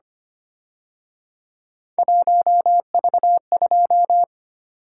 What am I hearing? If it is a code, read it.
Morse code, heard as a keyed tone, about 29 wpm, 700 Hz. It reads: 1V2
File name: Speed-Racing - Three Letter-Number Combinations - 25wpm